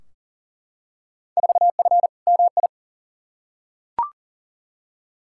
Morse code, heard as a keyed tone, about 40 wpm, 700 Hz. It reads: 4F MI
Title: Field Day Exchanges - 40wpm